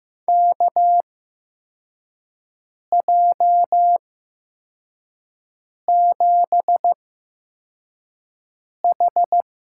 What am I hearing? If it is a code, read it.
Morse code, heard as a keyed tone, about 15 wpm, spaced as for 4 wpm, 700 Hz. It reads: KJ7H
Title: USA Call Signs - 6 Characters - 8x Character Spacing - 15wpm